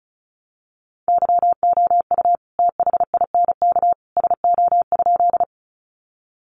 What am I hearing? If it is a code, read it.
Morse code, heard as a keyed tone, about 34 wpm, 700 Hz. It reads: YOU THINK SO?
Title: Sentences from Top 200 Words 35wpm